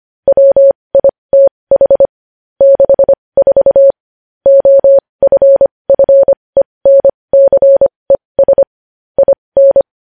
Morse code, heard as a keyed tone, about 25 wpm, spaced as for 20 wpm, 550 Hz. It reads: WITH 64 OFFENCES IN
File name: podcast / News headlines-1x-25x20